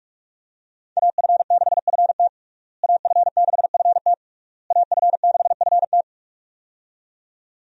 Morse code, heard as a keyed tone, about 41 wpm, 700 Hz. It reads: AF6FT AF6FT AF6FT
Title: USA Call Signs - 5 Characters - Repeated 3x - 45wpm